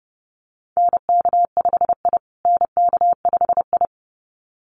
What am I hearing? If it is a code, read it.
Morse code, heard as a keyed tone, about 30 wpm, 700 Hz. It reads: NK5I NK5I
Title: USA Call Signs Repeated 2x 30wpm